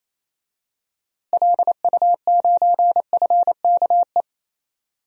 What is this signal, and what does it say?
Morse code, heard as a keyed tone, about 28 wpm, 700 Hz. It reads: LU9FKE
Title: DXCC-100 Call Signs - 6 Characters - Repeated 3x - 28wpm